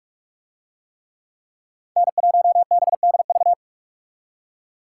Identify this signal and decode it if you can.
Morse code, heard as a keyed tone, about 41 wpm, 700 Hz. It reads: N1BDV